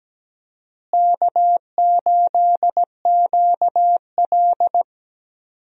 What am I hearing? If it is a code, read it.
Morse code, heard as a keyed tone, about 17 wpm, 700 Hz. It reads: K8QL
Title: USA Call Signs - 4 Characters - Repeated 3x - 17wpm